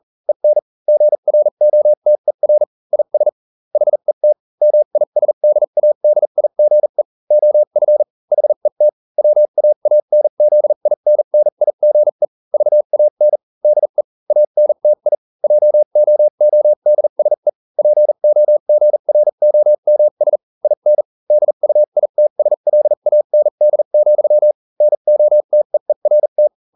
Morse code, heard as a keyed tone, about 40 wpm, spaced as for 35 wpm, 600 Hz. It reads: EN GROTER IS HET MISDADIGE OF HET WAANZINNIGE VAN DE ANTI JOODSE POGROMS IN DUITSLAND, NOTEERT